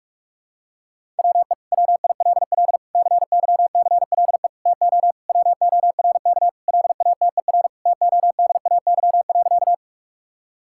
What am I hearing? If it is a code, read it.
Morse code, heard as a keyed tone, about 45 wpm, 700 Hz. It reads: WE WILL CYCLE TO WORK LATER TODAY.
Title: Sentences from Top 1000 Words - Review Part 6 - 45wpm